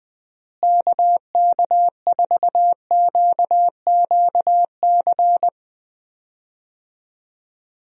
Morse code, heard as a keyed tone, about 20 wpm, 700 Hz. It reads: KK4QQC
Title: USA Call Signs - 6 Characters - Repeated 3x - 20wpm